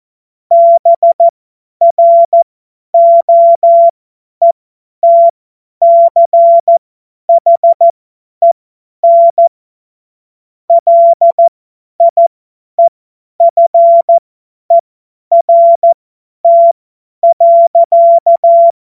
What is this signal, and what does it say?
Morse code, heard as a keyed tone, about 14 wpm, spaced as for 10 wpm, 680 Hz. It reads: BROETCHEN LIEFERT.